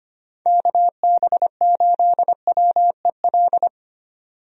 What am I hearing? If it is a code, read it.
Morse code, heard as a keyed tone, about 25 wpm, 700 Hz. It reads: KB8WEL
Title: USA Call Signs - 6 Characters - 25wpm